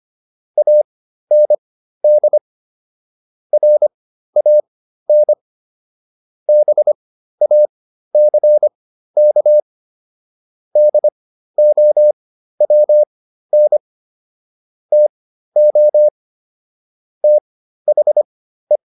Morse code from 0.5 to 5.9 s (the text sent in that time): AND RAN